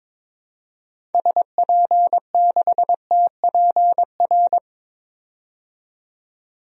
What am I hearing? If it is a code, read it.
Morse code, heard as a keyed tone, about 22 wpm, 700 Hz. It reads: SP6TPR
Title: DXCC-100 Call Signs - 6 Characters - 22wpm